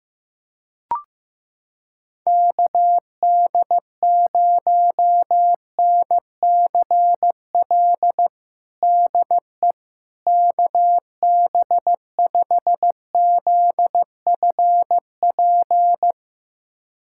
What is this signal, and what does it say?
Morse code, heard as a keyed tone, about 15 wpm, 700 Hz. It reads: KD0NCL DE KB5ZFP
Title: USA Callsign DE Callsign - 6 Characters - 15wpm